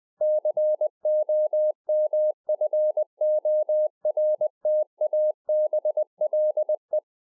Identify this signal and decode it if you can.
Morse code, heard as a keyed tone, about 20 wpm, 610 Hz. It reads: COMFORTABLE
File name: 20WPM